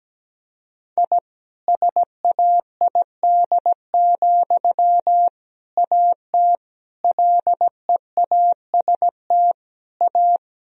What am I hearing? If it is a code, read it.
Morse code, heard as a keyed tone, about 17 wpm, 700 Hz. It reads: I SAID, AT LEAST A
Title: Sentences from Top 1000 Words - Review Part 2 - 17wpm